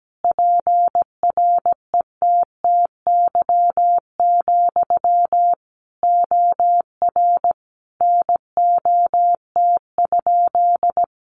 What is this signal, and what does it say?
Morse code, heard as a keyed tone, about 17 wpm, 700 Hz. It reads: PRETTY, OR NOT?
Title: Sentences from Top 1000 Words 17wpm